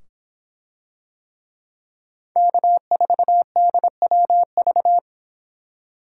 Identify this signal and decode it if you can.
Morse code, heard as a keyed tone, about 26 wpm, 700 Hz. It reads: K4DWV